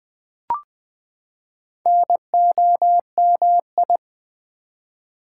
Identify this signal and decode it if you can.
Morse code, heard as a keyed tone, about 20 wpm, 700 Hz. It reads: NOMI
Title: Build A Word - 4 Letters At A Time - 10-Letter Words - 20wpm